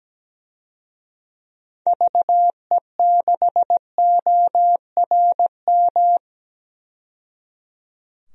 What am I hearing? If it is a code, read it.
Morse code, heard as a keyed tone, about 17 wpm, 700 Hz. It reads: VE6ORM
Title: DXCC-100 Call Signs - 6 Characters - 17wpm